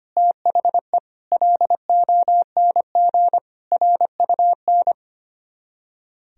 Morse code, heard as a keyed tone, about 31 wpm, 700 Hz. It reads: THE LONG RUN
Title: Speed-Racing - Sets of 4 Words 25wpm